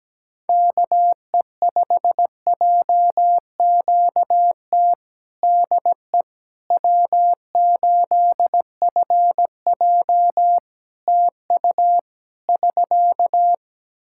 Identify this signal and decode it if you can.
Morse code, heard as a keyed tone, about 17 wpm, 700 Hz. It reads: KE5JQT DE W8FJ TU <SK>
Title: Practice QSO - Exchange 4 - Ending - 17wpm